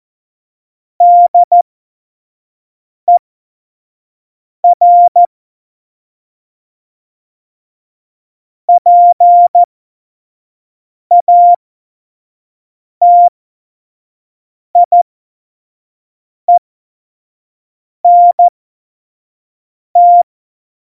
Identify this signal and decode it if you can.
Morse code, heard as a keyed tone, about 14 wpm, spaced as for 5 wpm, 700 Hz. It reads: DER PATIENT
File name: Morsekurs / Witze / Witz 05. Dezember 2017, 25 F70